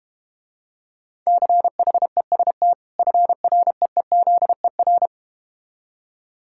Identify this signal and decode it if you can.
Morse code, heard as a keyed tone, about 31 wpm, 700 Hz. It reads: CHEST FREEZER